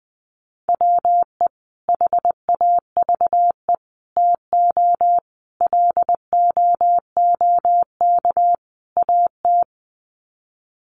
Morse code, heard as a keyed tone, about 20 wpm, 700 Hz. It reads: WE HAVE TO LOOK AT
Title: Sets of 5 Words Encore2 20wpm